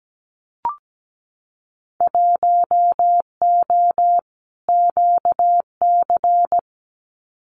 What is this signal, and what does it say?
Morse code, heard as a keyed tone, about 17 wpm, 700 Hz. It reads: 1O QC
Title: Winter Field Day Exchanges 17wpm